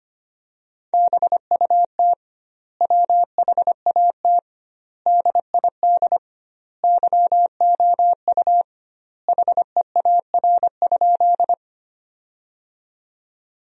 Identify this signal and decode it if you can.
Morse code, heard as a keyed tone, about 25 wpm, 700 Hz. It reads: BUT WHAT DID YOU HEAR?
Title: 2x Word-Spacing - Sentences from Top 300 Words - 25wpm